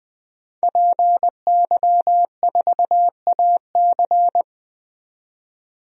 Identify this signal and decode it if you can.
Morse code, heard as a keyed tone, about 20 wpm, 700 Hz. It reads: PY4AC
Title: DXCC-100 Call Signs - 5 Characters - 20wpm